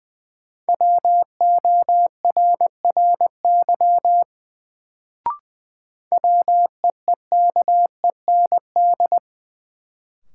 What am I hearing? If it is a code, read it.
Morse code, heard as a keyed tone, about 20 wpm, 700 Hz. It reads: WORRY WEEKEND
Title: Sets of 5 Words - New Word Drill - 20wpm